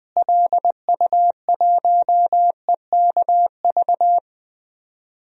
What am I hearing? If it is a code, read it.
Morse code, heard as a keyed tone, about 20 wpm, 700 Hz. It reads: LU1EKV
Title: DXCC-100 Call Signs - 6 Characters - Repeated 3x - 20wpm